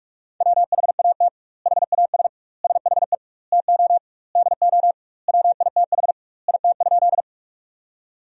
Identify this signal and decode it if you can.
Morse code, heard as a keyed tone, about 41 wpm, 700 Hz. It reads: WHAT HAS SHE TO DO WITH IT?
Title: 2x Word-Spacing - Sentences from Top 100 Words - 45wpm